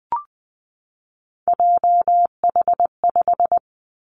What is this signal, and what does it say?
Morse code, heard as a keyed tone, about 20 wpm, 700 Hz. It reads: JH5
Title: Three Challenging Letter-Number Combinations 20wpm